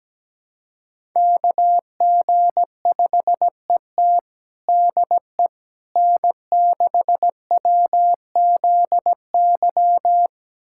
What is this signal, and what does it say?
Morse code, heard as a keyed tone, about 17 wpm, 700 Hz. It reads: KG5ET DE N6WZY
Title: USA Callsign DE Callsign - 5 Characters - 17wpm